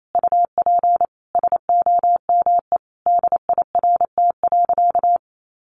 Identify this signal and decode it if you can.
Morse code, heard as a keyed tone, about 28 wpm, 700 Hz. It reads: UP SOME DIRT.
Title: A Sound Of Thunder 28wpm